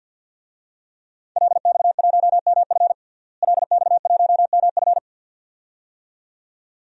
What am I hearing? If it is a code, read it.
Morse code, heard as a keyed tone, about 50 wpm, 700 Hz. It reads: LX1MF LX1MF
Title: DXCC-100 Call Signs - 5 Characters - Repeated 2x - 50wpm